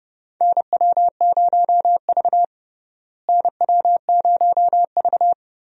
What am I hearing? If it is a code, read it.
Morse code, heard as a keyed tone, about 32 wpm, 700 Hz. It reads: NW0V NW0V
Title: USA Call Signs - 4 Characters - Repeated 2x - 30wpm